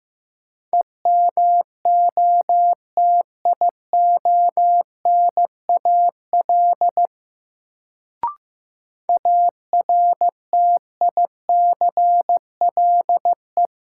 Morse code, heard as a keyed tone, about 15 wpm, 700 Hz. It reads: EMOTIONAL ARTICLE
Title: Sentences from Top 1000 Words - Review Part 5 - New Word Drill - 15wpm